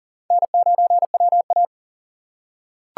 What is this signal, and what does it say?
Morse code, heard as a keyed tone, about 40 wpm, 700 Hz. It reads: N9WA
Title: USA Call Signs - 4 Characters - 4x Character Spacing - 40wpm